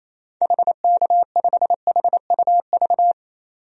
Morse code, heard as a keyed tone, about 28 wpm, 700 Hz. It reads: HK5HUV
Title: DXCC-100 Call Signs - 6 Characters - 28wpm